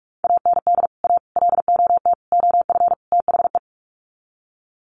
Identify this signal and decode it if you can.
Morse code, heard as a keyed tone, about 44 wpm, 700 Hz. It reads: AND A LOT OF THE
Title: Sets of 5 Words Encore 45wpm